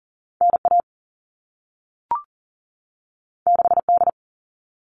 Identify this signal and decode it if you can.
Morse code, heard as a keyed tone, about 40 wpm, 700 Hz. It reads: NA 6D
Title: All Cut Numbers - 2-Digit 40wpm